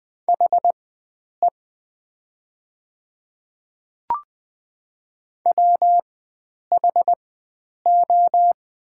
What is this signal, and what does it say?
Morse code, heard as a keyed tone, about 20 wpm, spaced as for 9 wpm, 700 Hz. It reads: HE WHO
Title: 4x Character Spacing - Top 100 Words - 20wpm